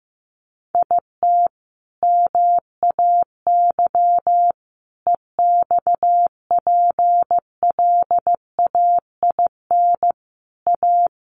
Morse code, heard as a keyed tone, about 15 wpm, 700 Hz. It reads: IT MAY EXPLAIN A
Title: Sentences from Top 1100 Words 15wpm